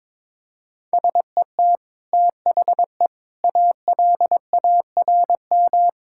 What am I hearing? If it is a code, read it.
Morse code, heard as a keyed tone, about 22 wpm, 700 Hz. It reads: SET THE ALARM
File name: practice / 22wpm / Easy Sentences with Hints - 22wpm